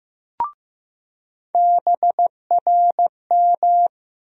Morse code, heard as a keyed tone, about 15 wpm, 700 Hz. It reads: BRM
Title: USA Call Sign Postfixes - 3 Letters 15wpm